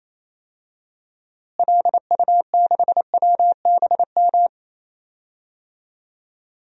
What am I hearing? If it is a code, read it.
Morse code, heard as a keyed tone, about 28 wpm, 700 Hz. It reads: LU6WBM
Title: DXCC-100 Call Signs - 6 Characters - Repeated 2x - 28wpm